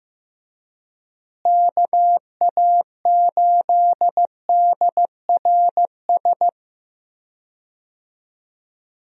Morse code, heard as a keyed tone, about 15 wpm, 700 Hz. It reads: KA8DRS
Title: USA Call Signs - 6 Characters - Repeated 2x - 15wpm